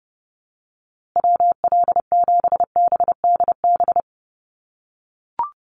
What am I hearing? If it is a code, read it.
Morse code, heard as a keyed tone, about 30 wpm, 700 Hz. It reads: WL7BDB
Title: USA Call Signs Repeated 2x 30wpm